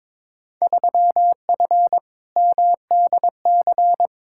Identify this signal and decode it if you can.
Morse code, heard as a keyed tone, about 22 wpm, 700 Hz. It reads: 3F MDC
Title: Field Day Exchanges Encore - 22wpm